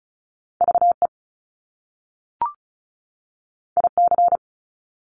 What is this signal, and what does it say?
Morse code, heard as a keyed tone, about 36 wpm, 700 Hz. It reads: VE IC
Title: Top Two Letter Combinations in Words 35wpm